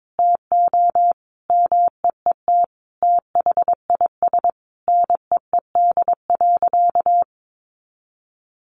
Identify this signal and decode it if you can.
Morse code, heard as a keyed tone, about 22 wpm, 700 Hz. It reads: TO MEET THIS NEED.
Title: Sentences from Top 1100 Words 22wpm